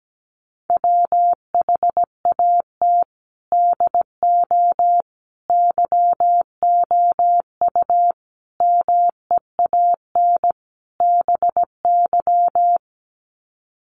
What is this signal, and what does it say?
Morse code, heard as a keyed tone, about 17 wpm, 700 Hz. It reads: WHAT DO YOU MEAN BY
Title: Sets of 5 Words 17wpm